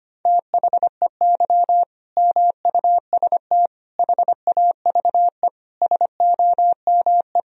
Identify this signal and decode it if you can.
Morse code, heard as a keyed tone, about 25 wpm, 700 Hz. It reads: THEY MUST HAVE SOME
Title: Sentences from Top 1000 Words - Review Part 3 - 25wpm